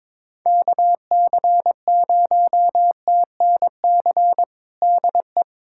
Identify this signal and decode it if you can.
Morse code, heard as a keyed tone, about 21 wpm, 700 Hz. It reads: KC0TNC DE
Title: USA Callsign DE Callsign - 6 Characters - 22wpm